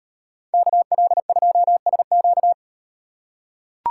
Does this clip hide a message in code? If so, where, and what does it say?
Yes: Morse code from 0.5 to 2.5 s, heard as a keyed tone, about 38 wpm, 700 Hz. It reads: KL2SQ